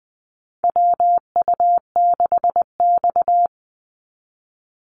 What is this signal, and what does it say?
Morse code, heard as a keyed tone, about 20 wpm, 700 Hz. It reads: WU6X
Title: Call Signs - 20wpm